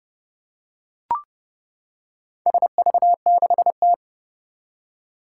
Morse code, heard as a keyed tone, about 30 wpm, 700 Hz. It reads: SV6T
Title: Four Challenging Letter-Number Combinations 30wpm